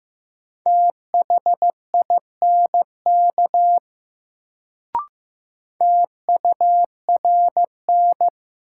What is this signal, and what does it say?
Morse code, heard as a keyed tone, about 15 wpm, 700 Hz. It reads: THINK TURN
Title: Sentences from Top 200 Words - New Word Drill - 15wpm